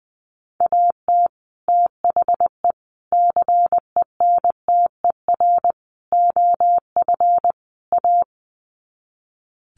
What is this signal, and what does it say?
Morse code, heard as a keyed tone, about 20 wpm, 700 Hz. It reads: AT THE CENTER OF A
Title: Sets of 5 Words Encore2 20wpm